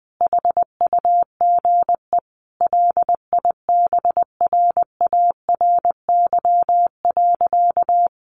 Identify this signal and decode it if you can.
Morse code, heard as a keyed tone, about 19 wpm, 700 Hz. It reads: HUGE LIBRARY.
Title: Sentences from Top 1100 Words 20wpm